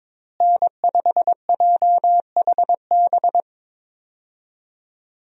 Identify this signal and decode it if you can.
Morse code, heard as a keyed tone, about 22 wpm, 700 Hz. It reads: N5JHB
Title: USA Call Signs - 5 Characters - Repeated 3x - 22wpm